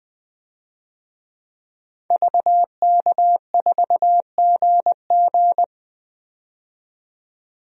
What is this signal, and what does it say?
Morse code, heard as a keyed tone, about 20 wpm, 700 Hz. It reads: VK4GG
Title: DXCC-100 Call Signs - 5 Characters - 20wpm